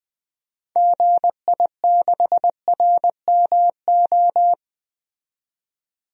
Morse code, heard as a keyed tone, about 20 wpm, 700 Hz. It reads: GI6RMO
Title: DXCC-100 Call Signs - 6 Characters - 20wpm